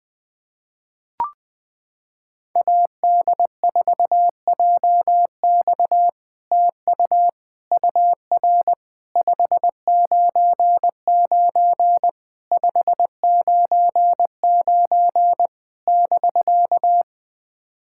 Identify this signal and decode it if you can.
Morse code, heard as a keyed tone, about 20 wpm, 700 Hz. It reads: AD4JX TU UR 599 599 <BK>
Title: POTA QSO Practice - 20wpm